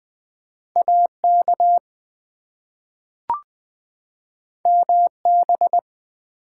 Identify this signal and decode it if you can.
Morse code, heard as a keyed tone, about 20 wpm, 700 Hz. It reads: AK MB
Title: ARRL Sections v2 - 20wpm